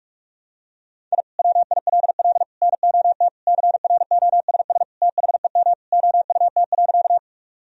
Morse code, heard as a keyed tone, about 46 wpm, 700 Hz. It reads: I WILL NOT CROSS THEM OUT.
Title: Sentences from Top 1000 Words - Review Part 1 - 45wpm